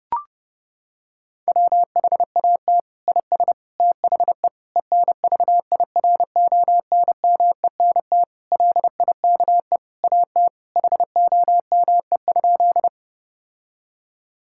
Morse code, heard as a keyed tone, about 30 wpm, 700 Hz. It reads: WHAT IS THE ENVIRONMENT LIKE AT HOME?
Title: Sentences from Top 1000 Words - Review Part 5 - 30wpm